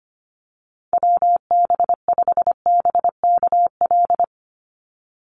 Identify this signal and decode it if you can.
Morse code, heard as a keyed tone, about 25 wpm, 700 Hz. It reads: WB5BKL
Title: NA SOTA Chaser Callsigns 25wpm